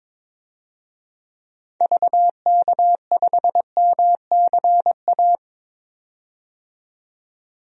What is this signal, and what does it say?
Morse code, heard as a keyed tone, about 22 wpm, 700 Hz. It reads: VK5MCA
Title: DXCC-100 Call Signs - 6 Characters - 22wpm